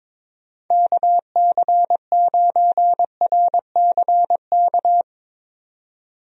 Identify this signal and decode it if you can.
Morse code, heard as a keyed tone, about 22 wpm, 700 Hz. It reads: KC9RCK
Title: USA Call Signs - 6 Characters - 22wpm